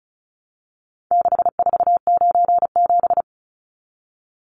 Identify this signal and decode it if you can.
Morse code, heard as a keyed tone, about 34 wpm, 700 Hz. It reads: 6497
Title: Numbers - 4-Digit 35wpm